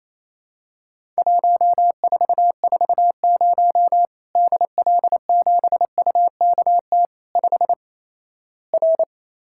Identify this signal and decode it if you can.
Morse code, heard as a keyed tone, about 28 wpm, 700 Hz. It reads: 1440 DL7UKT 5 R
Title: Worked All Europe DX Contest - QTC Exchange - 28wpm